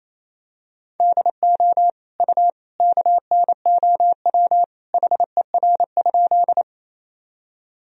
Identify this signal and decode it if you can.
Morse code, heard as a keyed tone, about 28 wpm, 700 Hz. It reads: DO U KNOW HER?
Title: Sentences from Top 100 Words and Abbreviations 28wpm